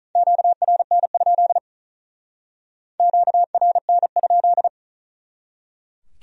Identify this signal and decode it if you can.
Morse code, heard as a keyed tone, about 43 wpm, 700 Hz. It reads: QRN? QRN?
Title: Speed-Racing Q-Codes - 35wpm